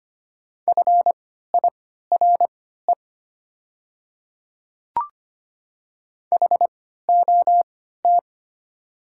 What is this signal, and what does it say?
Morse code, heard as a keyed tone, about 25 wpm, spaced as for 14 wpm, 700 Hz. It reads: FIRE HOT
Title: 3x Character Spacing - Top 500 Words - 25wpm